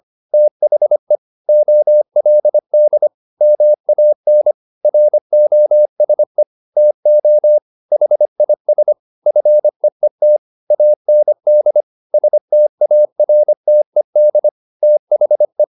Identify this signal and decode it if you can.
Morse code, heard as a keyed tone, about 25 wpm, 600 Hz. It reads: THE OLD MAN ROSE TO HIS FEET AND STARTED THE